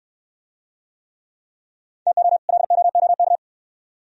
Morse code, heard as a keyed tone, about 67 wpm, 700 Hz. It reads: TO LOOK